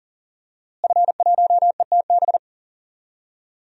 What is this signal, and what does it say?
Morse code, heard as a keyed tone, about 40 wpm, 700 Hz. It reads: F1ETB
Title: DXCC-100 Call Signs - 5 Characters - Repeated 3x - 40wpm